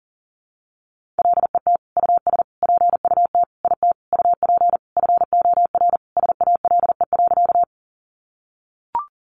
Morse code, heard as a keyed tone, about 40 wpm, 700 Hz. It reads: LET US PUT IT UP FOR SALE.
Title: Sentences from Top 1400 Words 40wpm